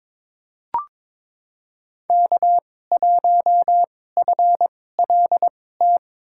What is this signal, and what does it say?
Morse code, heard as a keyed tone, about 22 wpm, spaced as for 16 wpm, 700 Hz. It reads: K1FLT
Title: USA Call Signs - 5 Characters - 2x Character Spacing - 22wpm